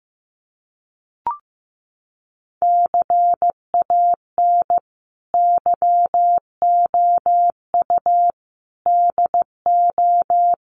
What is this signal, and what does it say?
Morse code, heard as a keyed tone, about 15 wpm, 700 Hz. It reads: CAN YOU DO
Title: Sentences from Top 100 Words 15wpm